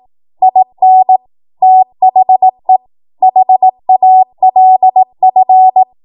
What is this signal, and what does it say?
Morse code, heard as a keyed tone, about 18 wpm, 750 Hz. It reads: IN THE HALF
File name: Morse / 18WPM